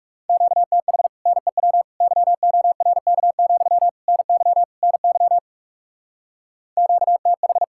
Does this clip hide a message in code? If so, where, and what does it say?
Yes: Morse code from 0.3 to 7.6 s, heard as a keyed tone, about 38 wpm, 700 Hz. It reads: QTH NEW YORK, NY NY QTH